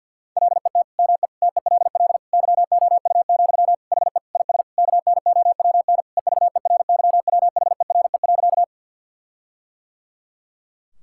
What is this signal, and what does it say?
Morse code, heard as a keyed tone, about 50 wpm, 700 Hz. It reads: LET ME TELL YOU, HE IS KNOWN EVERYWHERE.
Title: Sentences from Top 1600 Words 50wpm